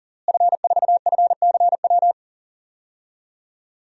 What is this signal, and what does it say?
Morse code, heard as a keyed tone, about 40 wpm, 700 Hz. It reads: F4FCW
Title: DXCC-100 Call Signs - 5 Characters - Repeated 3x - 40wpm